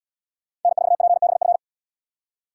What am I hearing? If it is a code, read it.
Morse code, heard as a keyed tone, about 77 wpm, 690 Hz. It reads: N4PBF